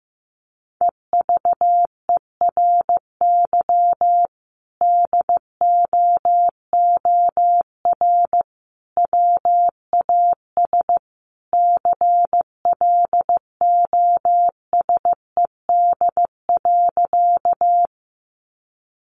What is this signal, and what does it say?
Morse code, heard as a keyed tone, about 15 wpm, 700 Hz. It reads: EVERY DOOR WAS CLOSED.
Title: Sentences from Top 1400 Words 15wpm